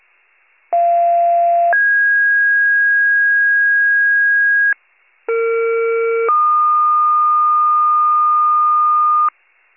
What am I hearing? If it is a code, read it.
Two-tone page, heard as a one-second tone, then a three-second tone, about 690 then 1650 Hz, about 460 then 1220 Hz, with noise. It is heard from 0.7 to 9.3 s.